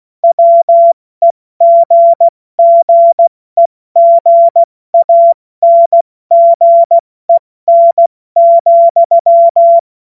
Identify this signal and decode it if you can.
Morse code, heard as a keyed tone, about 16 wpm, 680 Hz. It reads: WEGGEGANGEN,
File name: Morsekurs / Witze / Witz 10.Oktober 2023 70 F80